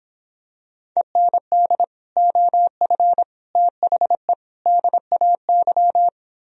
Morse code, heard as a keyed tone, about 26 wpm, 700 Hz. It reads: END OF THE DAY